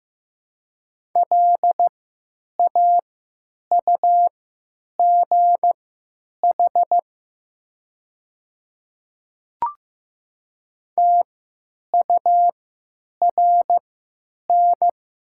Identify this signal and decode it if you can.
Morse code, heard as a keyed tone, about 15 wpm, spaced as for 9 wpm, 700 Hz. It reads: LAUGH TURN
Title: 3x Character Spacing - Top 500 Words - 15wpm